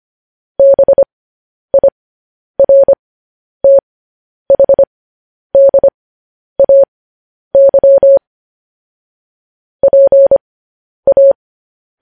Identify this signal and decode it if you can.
Morse code, heard as a keyed tone, about 25 wpm, spaced as for 10 wpm, 550 Hz. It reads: BIRTHDAY PA
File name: podcast / News headlines-1x-25x10